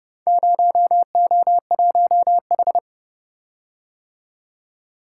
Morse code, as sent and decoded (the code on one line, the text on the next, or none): ----- --- .---- ....
0O1H